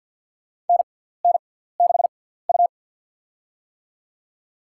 Morse code, heard as a keyed tone, about 52 wpm, spaced as for 17 wpm, 700 Hz. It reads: NN6U
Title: USA Call Signs - 4 Characters - 6x Character Spacing - 50wpm